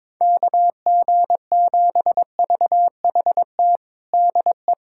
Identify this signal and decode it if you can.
Morse code, heard as a keyed tone, about 22 wpm, 700 Hz. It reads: KG7VHT DE
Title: USA Callsign DE Callsign - 6 Characters - 22wpm